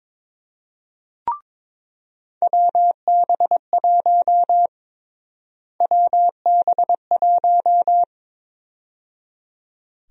Morse code, heard as a keyed tone, about 22 wpm, 700 Hz. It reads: WB1 WB1
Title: USA Call Sign Prefixes - Repeated 2x - 22wpm